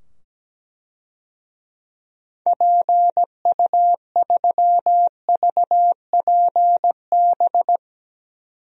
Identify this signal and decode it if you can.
Morse code, heard as a keyed tone, about 17 wpm, 700 Hz. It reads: PU3VPB
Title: DXCC-100 Call Signs - 6 Characters - 17wpm